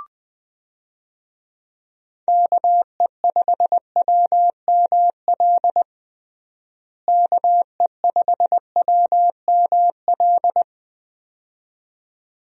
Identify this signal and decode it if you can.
Morse code, heard as a keyed tone, about 20 wpm, 700 Hz. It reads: KE5WML KE5WML
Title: USA Call Signs - 6 Characters - Repeated 2x - 20wpm